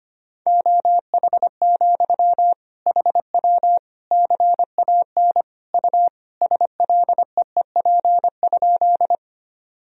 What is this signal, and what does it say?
Morse code, heard as a keyed tone, about 25 wpm, 700 Hz. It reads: OH, HW CAN U SLEEP?
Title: Sentences from Top 500 Words and Abbreviations 25wpm